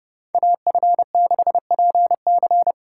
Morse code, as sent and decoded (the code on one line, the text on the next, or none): .- ..-. -.... .--. -.-.
AF6PC